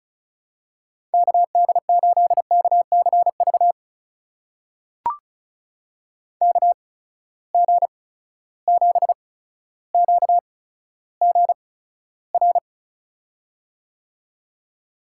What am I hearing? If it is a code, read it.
Morse code, heard as a keyed tone, about 36 wpm, spaced as for 15 wpm, 700 Hz. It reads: KD8KCV KG7QGR
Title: USA Call Signs - 6 Characters - 8x Character Spacing - 35wpm